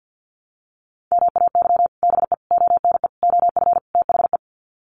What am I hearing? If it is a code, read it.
Morse code, heard as a keyed tone, about 50 wpm, 700 Hz. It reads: MAY BE ONE OF THE
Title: Sets of 5 Words Encore2 50wpm